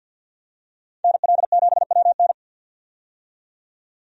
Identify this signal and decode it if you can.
Morse code, heard as a keyed tone, about 53 wpm, 700 Hz. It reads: NL7WN
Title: USA Call Signs - 5 Characters - 50wpm